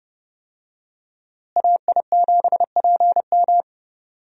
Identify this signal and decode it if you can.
Morse code, heard as a keyed tone, about 30 wpm, 700 Hz. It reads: AI7PM
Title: USA Call Signs - 5 Characters - Repeated 3x - 30wpm